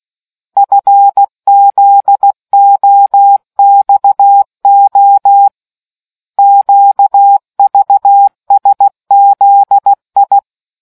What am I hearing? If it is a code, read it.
Morse code, heard as a keyed tone, about 16 wpm, 800 Hz. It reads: FZOXO QVSZI